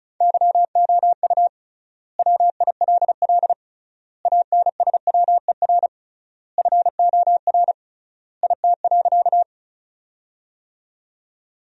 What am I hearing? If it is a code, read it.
Morse code, heard as a keyed tone, about 36 wpm, 700 Hz. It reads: YOU WILL ANSWER FOR IT.
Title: 3x Word-Spacing - Sentences from Top 300 Words - 35wpm